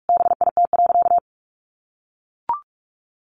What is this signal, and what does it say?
Morse code, heard as a keyed tone, about 44 wpm, 700 Hz. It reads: BIT.
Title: Sentences from Top 900 Words 45wpm